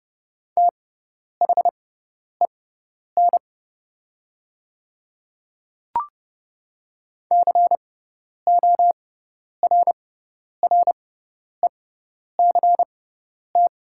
Morse code, heard as a keyed tone, about 30 wpm, spaced as for 10 wpm, 700 Hz. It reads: THEN CORRECT
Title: 6x Character Spacing - Top 500 Words - 30wpm